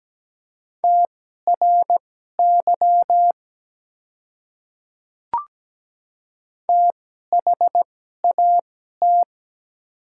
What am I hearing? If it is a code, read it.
Morse code, heard as a keyed tone, about 17 wpm, spaced as for 12 wpm, 700 Hz. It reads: TRY THAT
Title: 2x Character Spacing - Top 200 Words - 17wpm